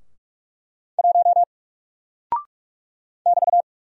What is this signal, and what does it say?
Morse code, heard as a keyed tone, about 46 wpm, 700 Hz. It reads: 1 <BT>
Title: Single Character (Letter:Number:Punctuation:Prosign) Alt 45wpm